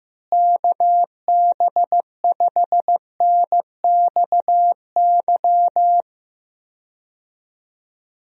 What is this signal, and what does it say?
Morse code, heard as a keyed tone, about 15 wpm, 700 Hz. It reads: KB5NXY
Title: USA Call Signs - 6 Characters - Repeated 3x - 15wpm